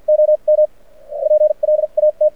Morse code, heard as a keyed tone, about 52 wpm, spaced as for 41 wpm, 610 Hz. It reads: OM PLAT
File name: archived / CRSnet